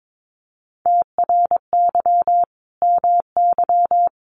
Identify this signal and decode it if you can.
Morse code, heard as a keyed tone, about 22 wpm, 700 Hz. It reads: TRY MY
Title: Sentences from Top 1200 Words 22wpm